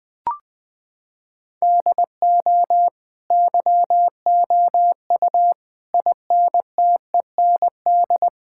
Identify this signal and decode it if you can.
Morse code, heard as a keyed tone, about 20 wpm, 700 Hz. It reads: DO YOU INTEND
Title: Sentences from Top 1600 Words 20wpm